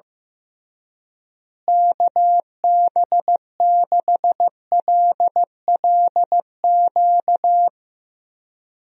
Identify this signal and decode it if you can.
Morse code, heard as a keyed tone, about 16 wpm, 700 Hz. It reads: KB6LLQ
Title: USA Call Signs - 6 Characters - Repeated 3x - 15wpm